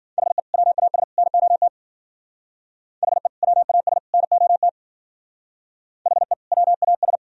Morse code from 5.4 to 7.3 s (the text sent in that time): HE WAS